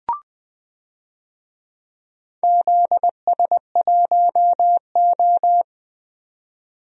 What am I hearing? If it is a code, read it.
Morse code, heard as a keyed tone, about 21 wpm, 700 Hz. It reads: ZS1O
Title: DXCC-100 Call Signs - 4 Characters - Repeated 3x - 20wpm